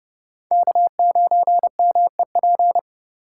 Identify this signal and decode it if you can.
Morse code, heard as a keyed tone, about 29 wpm, 700 Hz. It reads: K9MEP